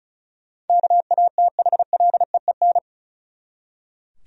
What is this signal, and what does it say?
Morse code, heard as a keyed tone, about 36 wpm, 700 Hz. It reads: KATHLEEN
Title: USA Common Names - 35wpm